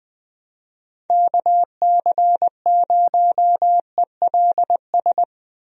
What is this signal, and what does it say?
Morse code, heard as a keyed tone, about 20 wpm, 700 Hz. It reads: KC0ELS
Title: USA Call Signs - 6 Characters - Repeated 3x - 20wpm